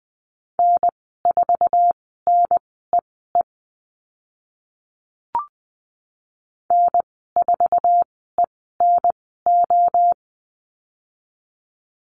Morse code, heard as a keyed tone, about 19 wpm, spaced as for 14 wpm, 700 Hz. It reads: N4NEE N4ENO
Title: 048 Lesson 9g - Number 4 - Call Signs 20wpm